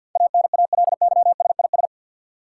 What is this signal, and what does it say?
Morse code, heard as a keyed tone, about 52 wpm, 700 Hz. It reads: ANALYSIS
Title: Sentences from Top 1000 Words - Review Part 5 - New Word Drill - 50wpm